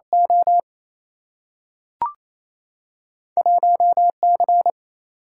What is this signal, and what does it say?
Morse code, heard as a keyed tone, about 28 wpm, 700 Hz. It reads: O 1C
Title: Single-Letter Single-Number 28wpm